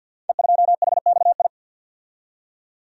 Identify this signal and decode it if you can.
Morse code, heard as a keyed tone, about 45 wpm, 700 Hz. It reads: E2HXI
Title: Code Groups - 5 Letter:Numbers - 50wpm